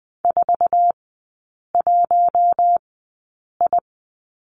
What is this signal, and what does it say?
Morse code, heard as a keyed tone, about 20 wpm, spaced as for 8 wpm, 700 Hz. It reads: 41I
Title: 053 Lesson 10e - Letter r - Three Characters 20wpm